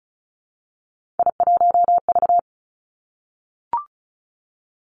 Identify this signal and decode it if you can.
Morse code, heard as a keyed tone, about 36 wpm, 700 Hz. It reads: I1V
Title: Three Challenging Letter-Number Combinations 35wpm